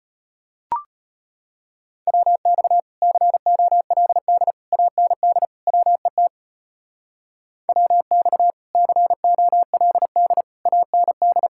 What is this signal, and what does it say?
Morse code, heard as a keyed tone, about 31 wpm, 700 Hz. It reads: WX COLD AND WET WX COLD AND
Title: Speed-Racing Common QSO Phrases - 25wpm